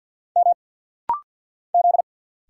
Morse code, heard as a keyed tone, about 50 wpm, 700 Hz. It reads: M Z
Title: Single Letter-Number - Rapid-Fire 50wpm